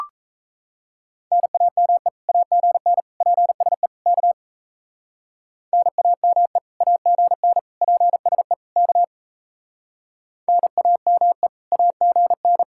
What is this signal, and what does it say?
Morse code, heard as a keyed tone, about 34 wpm, 700 Hz. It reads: NAME AGN PSE K NAME AGN PSE K NAME AGN
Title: Speed-Racing Common QSO Phrases - 28wpm